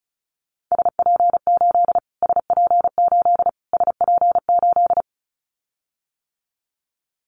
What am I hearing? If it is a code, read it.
Morse code, heard as a keyed tone, about 34 wpm, 700 Hz. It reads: SP8 SP8 SP8
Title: Callsign Prefixes for Top 25 DXCC - Repeated 3x - 35wpm